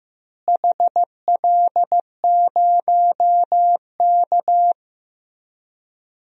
Morse code, heard as a keyed tone, about 15 wpm, 700 Hz. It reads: HL0K
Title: DXCC-100 Call Signs - 4 Characters - Repeated 3x - 15wpm